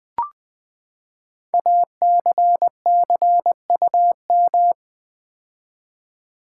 Morse code, heard as a keyed tone, about 20 wpm, 700 Hz. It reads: ACCUM
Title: Build A Word - 5 Letters At A Time - 12-Letter Words - 20wpm